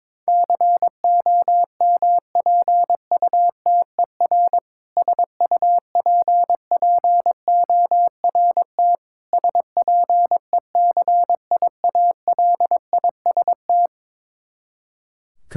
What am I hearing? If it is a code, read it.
Morse code, heard as a keyed tone, about 22 wpm, 700 Hz. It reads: COMPUTER SUPPORT SPECIALIST
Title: Practice QSO - Exchange 3 - More Info - 22wpm